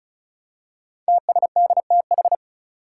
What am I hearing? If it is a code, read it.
Morse code, heard as a keyed tone, about 36 wpm, 700 Hz. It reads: TSDTH
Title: Five Challenging Letter-Number Combinations 35wpm